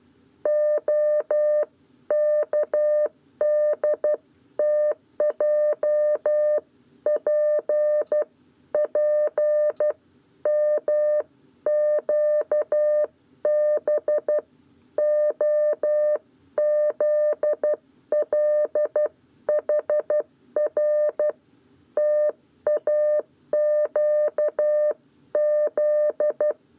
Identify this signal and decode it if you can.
Morse code, heard as a keyed tone, about 12 wpm, 600 Hz. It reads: OKDTJP P MQB OZLHR TAQZ
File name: 7b lezione cw 1